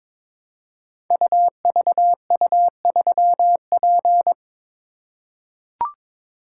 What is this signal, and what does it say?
Morse code, heard as a keyed tone, about 22 wpm, 700 Hz. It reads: UVU3P
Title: Code Groups - 5 Letter:Numbers - 22wpm